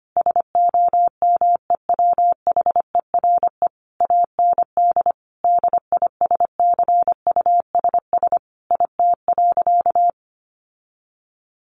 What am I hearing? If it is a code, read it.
Morse code, heard as a keyed tone, about 25 wpm, 700 Hz. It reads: SOMEWHERE AND DISCUSS IT.
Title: Sentences from Top 1400 Words 25wpm